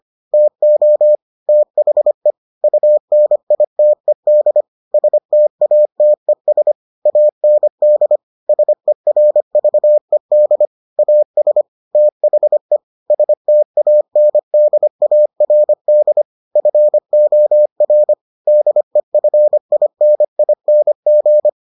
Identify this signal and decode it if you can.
Morse code, heard as a keyed tone, about 25 wpm, 600 Hz. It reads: TO THE UNITED STATES AND SERVED AS THE STANDARD FOR DEFINING